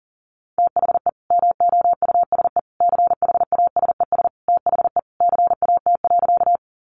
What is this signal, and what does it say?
Morse code, heard as a keyed tone, about 40 wpm, 700 Hz. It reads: THE MOUSE CHASES THE CAT.
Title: Easy Phrases 40wpm